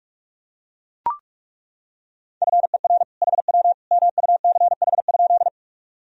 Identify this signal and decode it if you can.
Morse code, heard as a keyed tone, about 41 wpm, 700 Hz. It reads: FER HW MUCH?
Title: Sentences from Top 200 Words and Abbreviations 45wpm